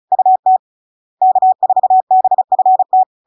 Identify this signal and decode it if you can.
Morse code, heard as a keyed tone, about 36 wpm, 750 Hz. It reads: UT K4BFT
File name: LARK-FD-2022-Log-in-CW-35wpm-750-Hz